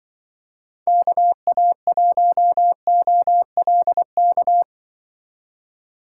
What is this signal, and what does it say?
Morse code, heard as a keyed tone, about 24 wpm, 700 Hz. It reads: KA1OLK
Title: Speed Racing - USA 2x3 Call Signs 20wpm